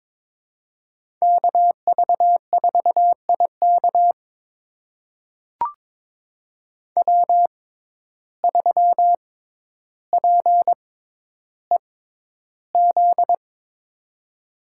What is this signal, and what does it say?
Morse code, heard as a keyed tone, about 22 wpm, spaced as for 11 wpm, 700 Hz. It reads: KV4IK W3PEZ
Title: USA Call Signs - 5 Characters - 6x Character Spacing - 22wpm